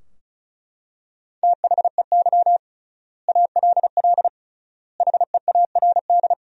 Morse code, heard as a keyed tone, about 34 wpm, 700 Hz. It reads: THEY ALL HEARD
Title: 3x Word-Spacing - Sentences from Top 400 Words - 35wpm